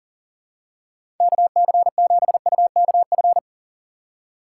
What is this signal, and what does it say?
Morse code, heard as a keyed tone, about 43 wpm, 700 Hz. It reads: KC7UKF
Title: USA Call Signs - 4x Character Spacing - 40wpm